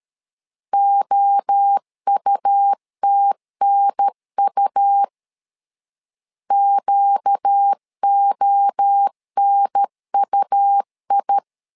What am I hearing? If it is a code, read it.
Morse code, heard as a keyed tone, about 13 wpm, 800 Hz. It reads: OUTNU QONUI
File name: LEZIONI / 4 Q 12wpm 1833